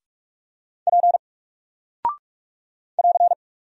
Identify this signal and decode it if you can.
Morse code, heard as a keyed tone, about 44 wpm, 700 Hz. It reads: P <AR>
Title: Single Character (Letter:Number:Punctuation:Prosign) 45wpm